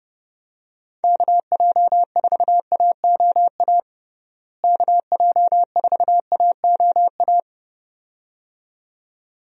Morse code, heard as a keyed tone, about 30 wpm, 700 Hz. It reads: KJ4AOA KJ4AOA
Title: USA Call Signs - 6 Characters - Repeated 2x - 30wpm